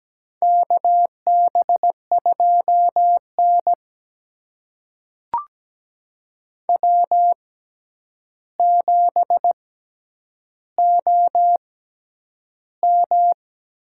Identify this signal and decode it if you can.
Morse code, heard as a keyed tone, about 17 wpm, spaced as for 9 wpm, 700 Hz. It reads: KB2N W7OM
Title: USA Call Signs - 4 Characters - 6x Character Spacing - 17wpm